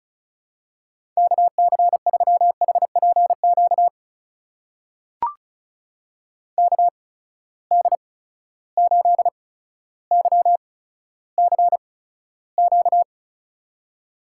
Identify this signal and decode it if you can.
Morse code, heard as a keyed tone, about 36 wpm, spaced as for 15 wpm, 700 Hz. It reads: KC3HPQ KD8YCQ
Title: USA Call Signs - 6 Characters - 8x Character Spacing - 35wpm